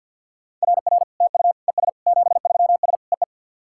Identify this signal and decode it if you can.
Morse code, heard as a keyed tone, about 50 wpm, 700 Hz. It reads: RR TU ES 73S EE
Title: SOTA QSO Practice 50wpm